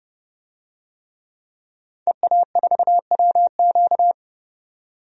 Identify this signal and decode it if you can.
Morse code, heard as a keyed tone, about 31 wpm, 700 Hz. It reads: EA4WQ
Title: DXCC-100 Call Signs - 5 Characters - 30wpm